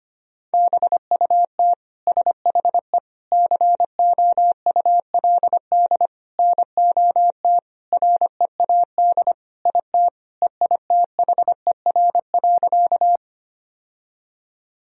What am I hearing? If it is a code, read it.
Morse code, heard as a keyed tone, about 25 wpm, 700 Hz. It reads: BUT SHE COULD NOT READ IT EITHER.
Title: Sentences from Top 1000 Words - Review Part 4 - 25wpm